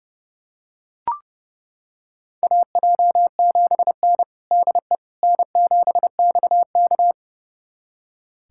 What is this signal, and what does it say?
Morse code, heard as a keyed tone, about 30 wpm, 700 Hz. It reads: AJ7N DE N7XK
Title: USA Callsign DE Callsign - 4 Characters - 30wpm